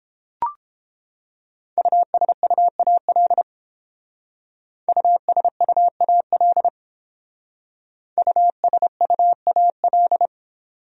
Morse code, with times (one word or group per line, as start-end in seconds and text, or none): 1.8-3.4 USUAL
4.9-6.7 USUAL
8.2-10.3 USUAL